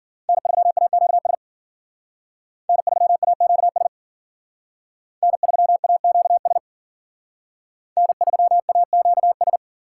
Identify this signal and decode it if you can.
Morse code, heard as a keyed tone, about 52 wpm, 700 Hz. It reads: N3AQS N3AQS N3AQS N3AQS